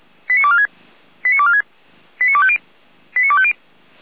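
Five-tone selective call (ZVEI), heard as quick runs of five tones, tones of about 70 ms. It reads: 89246 89246 89250 89250
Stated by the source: ZVEI five-tone selective call received by SDR, from feuerwehr-orlamuende (Orlamünde fire brigade) audio archive.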